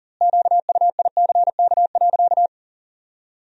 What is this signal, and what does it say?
Morse code, heard as a keyed tone, about 40 wpm, 700 Hz. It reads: QUICK.
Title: 2x Word-Spacing - Sentences from Top 500 Words - 40wpm